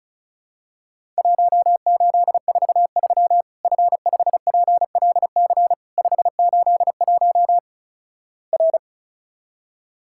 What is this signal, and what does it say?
Morse code, heard as a keyed tone, about 36 wpm, 700 Hz. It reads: 1843 F5PLC 581 R
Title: Worked All Europe DX Contest - QTC Exchange - 35wpm